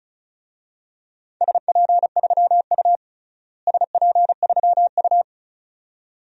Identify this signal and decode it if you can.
Morse code, heard as a keyed tone, about 36 wpm, 700 Hz. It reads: SP3U SP3U
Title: DXCC-100 Call Signs - 4 Characters - Repeated 2x - 35wpm